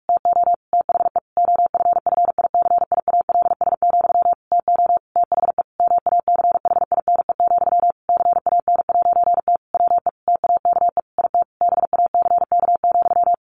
Morse code, heard as a keyed tone, about 45 wpm, 700 Hz. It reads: TO THE OFFICIALS, TO THE MACHINE, CAN'T WE TAKE IT BACK,
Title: A Sound Of Thunder 45wpm